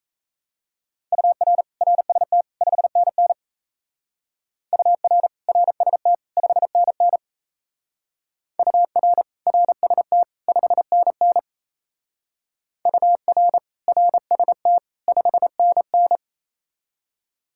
Morse code, heard as a keyed tone, about 36 wpm, 700 Hz. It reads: UR RST 5NN UR RST 5NN UR RST 5NN UR RST 5NN